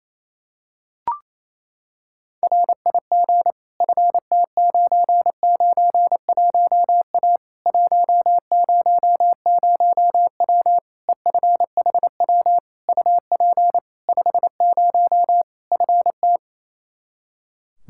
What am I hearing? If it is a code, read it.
Morse code, heard as a keyed tone, about 28 wpm, 700 Hz. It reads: RIG FT991A 100W EFHW UP 50 FT
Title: Practice QSO - Exchange 2 - Info - 28wpm